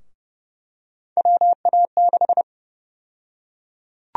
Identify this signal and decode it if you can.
Morse code, heard as a keyed tone, about 30 wpm, 700 Hz. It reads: WA6
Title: USA Call Sign Prefixes - Repeated 2x - 30wpm